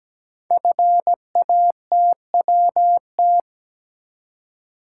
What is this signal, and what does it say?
Morse code, heard as a keyed tone, about 17 wpm, 700 Hz. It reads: FATWT
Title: Code Groups - 5 Letters - 17wpm